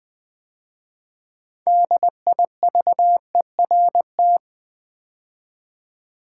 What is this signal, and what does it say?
Morse code, heard as a keyed tone, about 20 wpm, 700 Hz. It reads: DIVERT
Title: 6 Letter Words - First 3 Letters Given - 20wpm